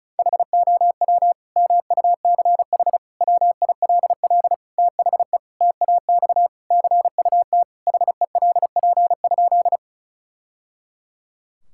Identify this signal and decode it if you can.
Morse code, heard as a keyed tone, about 36 wpm, 700 Hz. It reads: HOW MUCH WILL THE TAX CUT HELP?
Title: Sentences from Top 1000 Words - Review Part 5 - 35wpm